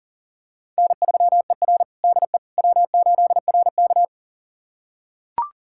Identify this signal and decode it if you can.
Morse code, heard as a keyed tone, about 37 wpm, 700 Hz. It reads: N3ER DE W8RK
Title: USA Callsign DE Callsign - 4 Characters - 40wpm